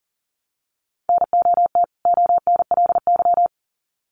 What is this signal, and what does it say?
Morse code, heard as a keyed tone, about 40 wpm, 700 Hz. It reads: NOT ONLY
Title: Sets of 2 Words 40wpm